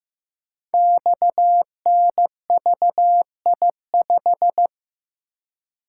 Morse code, heard as a keyed tone, about 15 wpm, 700 Hz. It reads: XNVI5
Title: Code Groups - 5 Letter:Numbers - 15wpm